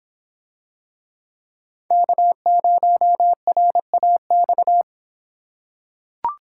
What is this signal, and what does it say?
Morse code, heard as a keyed tone, about 25 wpm, 700 Hz. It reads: K0RAX